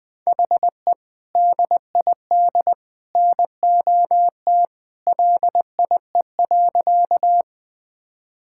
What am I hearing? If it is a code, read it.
Morse code, heard as a keyed tone, about 20 wpm, 700 Hz. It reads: HE DID NOT LIE.
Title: Sentences from Top 1600 Words 20wpm